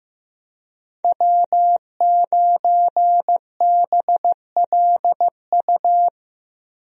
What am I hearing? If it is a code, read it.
Morse code, heard as a keyed tone, about 15 wpm, 700 Hz. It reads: W9BLU
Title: USA Call Signs - 5 Characters - Repeated 2x - 15wpm